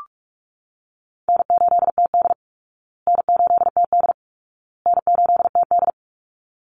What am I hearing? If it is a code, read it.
Morse code, heard as a keyed tone, about 44 wpm, 700 Hz. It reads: N8TD N8TD N8TD
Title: Easy Call Signs Repeated 3x - 45wpm